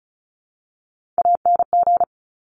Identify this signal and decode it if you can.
Morse code, heard as a keyed tone, about 36 wpm, 700 Hz. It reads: ANG
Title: Top Three Letter Combinations in Words 35wpm